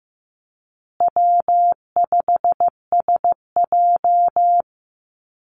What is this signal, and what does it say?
Morse code, heard as a keyed tone, about 15 wpm, 700 Hz. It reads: W5SJ
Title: Call Signs Repeated 2x - 15wpm